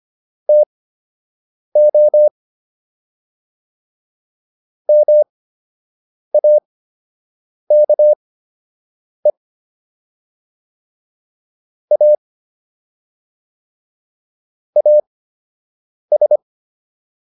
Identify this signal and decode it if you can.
Morse code, heard as a keyed tone, about 25 wpm, spaced as for 7 wpm, 600 Hz. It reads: TO MAKE A AS